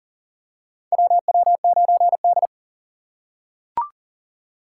Morse code, heard as a keyed tone, about 40 wpm, 700 Hz. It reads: WW9D
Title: USA Call Signs - 4x Character Spacing - 40wpm